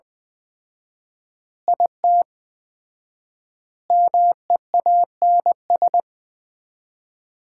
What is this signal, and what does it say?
Morse code, heard as a keyed tone, about 20 wpm, 700 Hz. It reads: IT MEANS
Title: 4x Word-Spacing - Sets of 5 Words - 20wpm